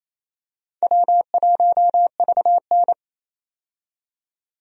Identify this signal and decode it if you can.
Morse code, heard as a keyed tone, about 28 wpm, 700 Hz. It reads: W1VN
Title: USA Call Signs - 4 Characters - 28wpm